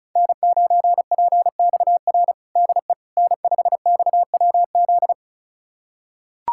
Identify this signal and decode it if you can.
Morse code, heard as a keyed tone, about 34 wpm, 700 Hz. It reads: N9PXR DE N5XWZ
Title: USA Callsign DE Callsign - 5 Characters - 35wpm